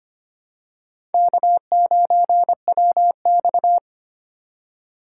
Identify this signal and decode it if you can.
Morse code, heard as a keyed tone, about 25 wpm, 700 Hz. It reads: K9WX
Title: USA Call Signs - 4 Characters - 3x Character Spacing - 25wpm